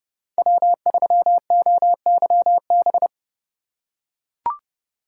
Morse code, heard as a keyed tone, about 30 wpm, 700 Hz. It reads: W3OYB
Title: USA Call Signs - 5 Characters - 3x Character Spacing - 30wpm